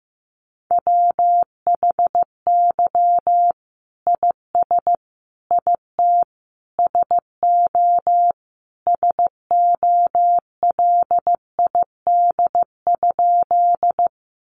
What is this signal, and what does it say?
Morse code, heard as a keyed tone, about 15 wpm, 700 Hz. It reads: WHY IS IT SO SOLID?
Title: Sentences from Top 1500 Words 15wpm